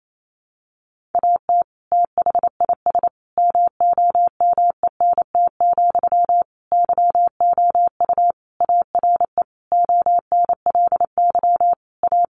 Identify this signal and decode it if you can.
Morse code, heard as a keyed tone, about 28 wpm, 700 Hz. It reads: AT THIS MOMENT, YOU ARE ONLY A
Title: Sentences from Top 1000 Words 28wpm